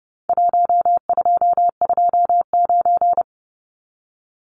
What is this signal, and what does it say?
Morse code, heard as a keyed tone, about 30 wpm, 700 Hz. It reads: 1229
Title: Numbers - Contest Style - 4-Digit 30wpm